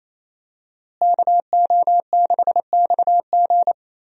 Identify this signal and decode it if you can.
Morse code, heard as a keyed tone, about 27 wpm, 700 Hz. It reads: KO6XG
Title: USA Call Signs - 5 Characters - Repeated 3x - 28wpm